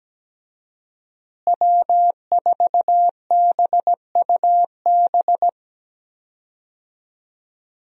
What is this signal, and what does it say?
Morse code, heard as a keyed tone, about 17 wpm, 700 Hz. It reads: W4BUB
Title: USA Call Signs - 5 Characters - Repeated 3x - 17wpm